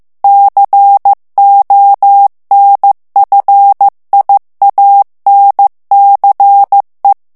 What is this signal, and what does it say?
Morse code, heard as a keyed tone, about 15 wpm, 800 Hz. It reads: CONFIANCE
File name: FRANCAIS TEXTE CLAIR 2 VITESSE 14 MOTS MINUTES